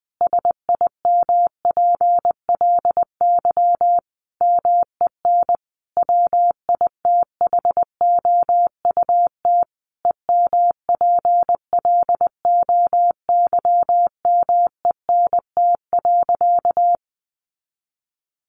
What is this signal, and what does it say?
Morse code, heard as a keyed tone, about 20 wpm, 700 Hz. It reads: SIMPLY MEN WITHOUT EMPLOYMENT.
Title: Sentences from Top 1500 Words 20wpm